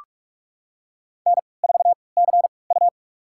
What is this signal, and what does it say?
Morse code, heard as a keyed tone, about 47 wpm, spaced as for 26 wpm, 700 Hz. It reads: N4CU
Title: USA Call Signs - 3x Character Spacing - 45wpm